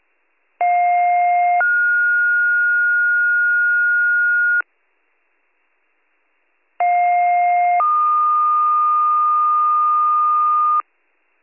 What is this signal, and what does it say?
Two-tone page, heard as a one-second tone, then a three-second tone, about 710 then 1430 Hz, about 710 then 1220 Hz. It is heard from 0.6 to 10.8 s.